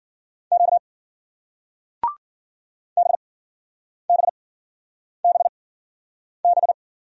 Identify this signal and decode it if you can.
Morse code, heard as a keyed tone, about 56 wpm, spaced as for 9 wpm, 700 Hz. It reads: Q BBBB